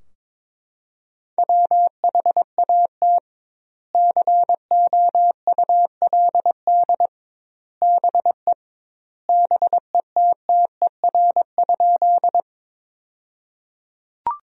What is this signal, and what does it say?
Morse code, heard as a keyed tone, about 21 wpm, 700 Hz. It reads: WHAT COULD BE BETTER?
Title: 2x Word-Spacing - Sentences from Top 400 Words - 22wpm